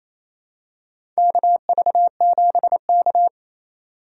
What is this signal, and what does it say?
Morse code, heard as a keyed tone, about 28 wpm, 700 Hz. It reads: KV7K
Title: USA Call Signs - 4 Characters - 8x Character Spacing - 28wpm